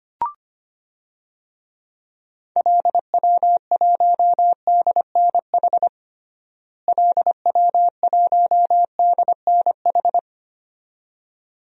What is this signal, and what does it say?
Morse code, heard as a keyed tone, about 25 wpm, 700 Hz. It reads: LW1DNH LW1DNH
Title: DXCC-100 Call Signs - 6 Characters - Repeated 2x - 25wpm